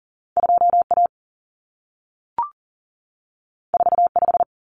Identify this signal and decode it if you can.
Morse code, heard as a keyed tone, about 40 wpm, 700 Hz. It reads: 2A 45
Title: Common Cut Numbers - 2-Digit 40wpm